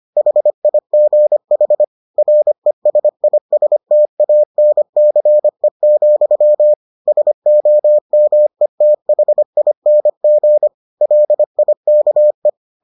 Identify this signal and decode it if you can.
Morse code, heard as a keyed tone, about 25 wpm, 600 Hz. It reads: HIGH RESISTANCE, SOMETHING LIKE